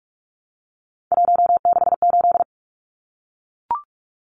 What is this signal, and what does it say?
Morse code, heard as a keyed tone, about 44 wpm, 700 Hz. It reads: 168
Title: Numbers - 3-Digit 45wpm